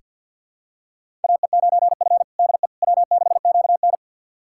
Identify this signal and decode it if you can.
Morse code, heard as a keyed tone, about 48 wpm, 700 Hz. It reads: AE9F DE W6QN
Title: USA Callsign DE Callsign - 4 Characters - 50wpm